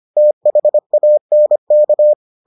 Morse code, heard as a keyed tone, about 25 wpm, 600 Hz. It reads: THANK